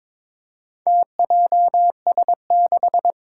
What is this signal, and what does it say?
Morse code, heard as a keyed tone, about 22 wpm, 700 Hz. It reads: TJS6
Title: Four Challenging Letter-Number Combinations 22wpm